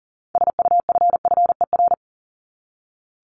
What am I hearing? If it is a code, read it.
Morse code, heard as a keyed tone, about 39 wpm, 700 Hz. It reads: SUFFER
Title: ING Suffix - Top 100 Words with Base Words 40wpm